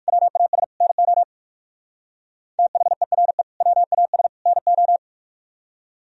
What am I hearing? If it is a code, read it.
Morse code, heard as a keyed tone, about 58 wpm, 700 Hz. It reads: WAS NO THERE WAS NO